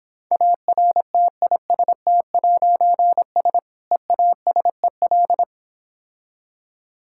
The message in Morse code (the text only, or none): ARTIST'S EASEL